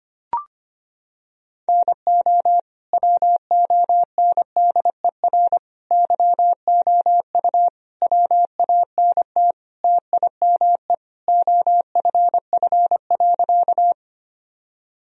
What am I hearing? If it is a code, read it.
Morse code, heard as a keyed tone, about 25 wpm, 700 Hz. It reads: NO WONDER YOU WANT TIME OFF.
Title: Sentences from Top 1000 Words - Review Part 3 - 25wpm